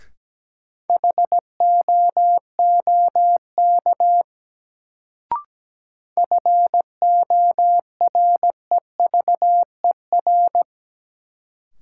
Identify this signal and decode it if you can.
Morse code, heard as a keyed tone, about 17 wpm, 700 Hz. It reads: HOOK FOREVER
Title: Sentences from Top 1600 Words - New Word Drill Alt - 17wpm